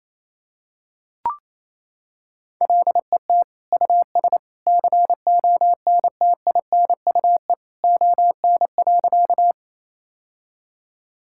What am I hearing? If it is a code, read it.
Morse code, heard as a keyed tone, about 27 wpm, 700 Hz. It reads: LET US CONTINUE ON.
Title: Sentences from Top 1000 Words - Review Part 6 - 28wpm